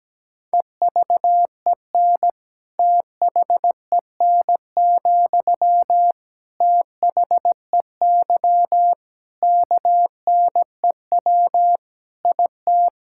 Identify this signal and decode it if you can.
Morse code, heard as a keyed tone, about 17 wpm, 700 Hz. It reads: EVEN THEN, THEY KNEW IT
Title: Sentences from Top 1600 Words 17wpm